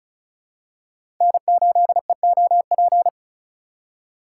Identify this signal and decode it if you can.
Morse code, heard as a keyed tone, about 36 wpm, 700 Hz. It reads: N8EOP
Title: USA Call Signs - 3x Character Spacing - 35wpm